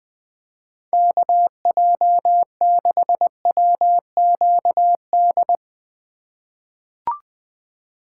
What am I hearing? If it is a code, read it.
Morse code, heard as a keyed tone, about 20 wpm, 700 Hz. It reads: KJ6WQD
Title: USA Call Signs - 3x Character Spacing - 20wpm